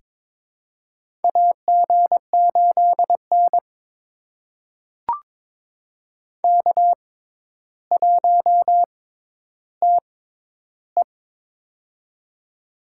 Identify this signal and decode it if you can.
Morse code, heard as a keyed tone, about 22 wpm, spaced as for 11 wpm, 700 Hz. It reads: AG8N K1TE
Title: USA Call Signs - 4 Characters - 6x Character Spacing - 22wpm